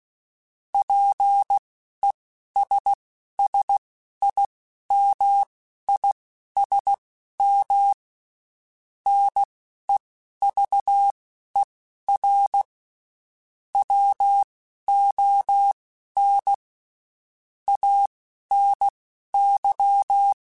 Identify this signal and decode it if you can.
Morse code, heard as a keyed tone, about 16 wpm, spaced as for 11 wpm, 800 Hz. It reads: PESSIMISM NEVER WON ANY
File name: Quotes Spaced / Quotes Spaced-mp3s / Pessimism sp @15